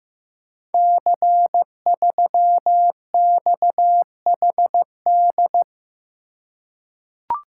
Morse code, heard as a keyed tone, about 15 wpm, 700 Hz. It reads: C3XHD
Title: Code Groups - 5 Letter:Numbers - 15wpm